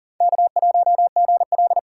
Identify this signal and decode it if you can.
Morse code, heard as a keyed tone, about 40 wpm, 700 Hz. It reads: K1GL
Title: USA Callsign DE Callsign - 4 Characters - 40wpm